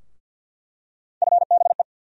Morse code, heard as a keyed tone, about 46 wpm, 700 Hz. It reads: FBE
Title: USA Call Sign Postfixes - 3 Letters 50wpm